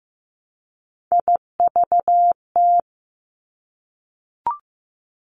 Morse code, heard as a keyed tone, about 15 wpm, 700 Hz. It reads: IVT
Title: Three Challenging Letter-Number Combinations 15wpm